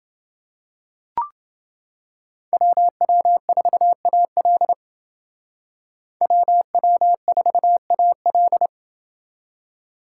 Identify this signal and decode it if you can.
Morse code, heard as a keyed tone, about 29 wpm, 700 Hz. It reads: WW4AL WW4AL